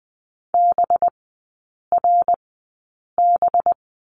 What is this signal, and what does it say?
Morse code, heard as a keyed tone, about 20 wpm, spaced as for 8 wpm, 700 Hz. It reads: BRB
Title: 184 Lesson 30e - Letter b - Three Characters 20wpm